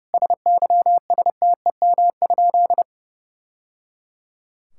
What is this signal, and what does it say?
Morse code, heard as a keyed tone, about 30 wpm, 700 Hz. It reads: SYSTEM?
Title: Sentences from Top 1000 Words - Review Part 6 - 30wpm